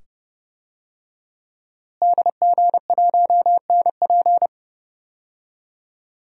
Morse code, heard as a keyed tone, about 30 wpm, 700 Hz. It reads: DG1NP
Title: DXCC-100 Call Signs - 5 Characters - Repeated 2x - 30wpm